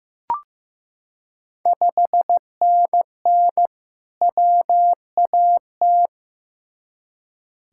Morse code, heard as a keyed tone, about 16 wpm, 700 Hz. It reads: 5NN WAT
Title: Ontario QSO Party Exchange 15wpm